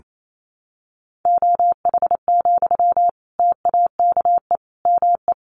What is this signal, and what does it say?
Morse code, heard as a keyed tone, about 28 wpm, 700 Hz. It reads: OH, TAKE ME
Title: Sentences from Top 500 Words 28wpm